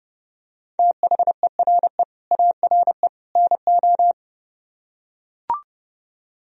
Morse code, heard as a keyed tone, about 28 wpm, 700 Hz. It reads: THERE ARE NO